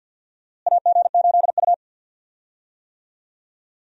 Morse code, heard as a keyed tone, about 52 wpm, 700 Hz. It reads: AG8U
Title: USA Call Signs - 4 Characters - Repeated 3x - 50wpm